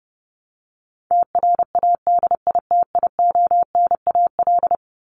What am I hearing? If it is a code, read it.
Morse code, heard as a keyed tone, about 30 wpm, 700 Hz. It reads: TRADITIONAL
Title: Top 751-1000 Words 30wpm